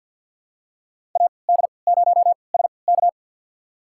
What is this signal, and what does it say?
Morse code, heard as a keyed tone, about 46 wpm, spaced as for 27 wpm, 700 Hz. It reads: AD0SK
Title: USA Call Signs - 5 Characters - 3x Character Spacing - 50wpm